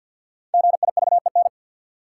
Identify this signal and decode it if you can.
Morse code, heard as a keyed tone, about 52 wpm, 700 Hz. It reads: GIVEN
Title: Sentences from Top 1400 Words - New Word Drill - 50wpm